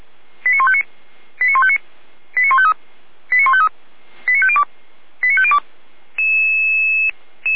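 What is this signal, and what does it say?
Five-tone selective call (ZVEI), heard as quick runs of five tones, tones of about 70 ms. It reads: 89159 89159 89153 89153 89502 89502
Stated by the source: Five-tone ZVEI selective call received by SDR, from the fire brigade (Feuerwehr) Orlamünde recording server.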